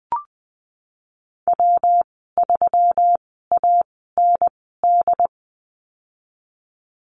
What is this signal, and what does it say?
Morse code, heard as a keyed tone, about 20 wpm, spaced as for 15 wpm, 700 Hz. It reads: W3AND
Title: 140 Lesson 23g - Letter f - Call Signs 20wpm